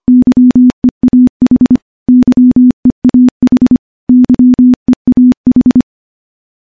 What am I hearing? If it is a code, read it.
Morse code, heard as a keyed tone, about 25 wpm, 250 Hz. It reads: YEAH YEAH YEAH
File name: morse-training-25WPM